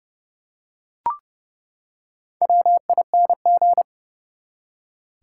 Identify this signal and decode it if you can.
Morse code, heard as a keyed tone, about 31 wpm, 700 Hz. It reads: WING
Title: Four Letter Words - 30wpm